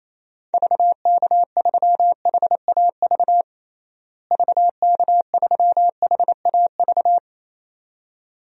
Morse code, heard as a keyed tone, about 27 wpm, 700 Hz. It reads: VK3HAV VK3HAV
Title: DXCC-100 Call Signs - 6 Characters - Repeated 2x - 28wpm